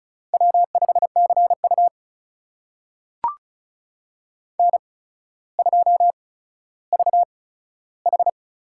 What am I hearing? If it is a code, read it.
Morse code, heard as a keyed tone, about 34 wpm, spaced as for 15 wpm, 700 Hz. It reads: W5CU N2VH
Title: USA Call Signs - 4 Characters - 8x Character Spacing - 35wpm